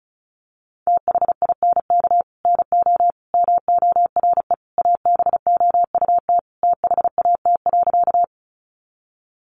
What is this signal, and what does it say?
Morse code, heard as a keyed tone, about 34 wpm, 700 Hz. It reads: THINK NO MORE ABOUT THAT.
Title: Sentences from Top 200 Words 35wpm